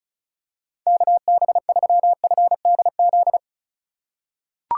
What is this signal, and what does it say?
Morse code, heard as a keyed tone, about 36 wpm, 700 Hz. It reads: KB3FDZ
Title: USA Call Signs - 6 Characters - 3x Character Spacing - 35wpm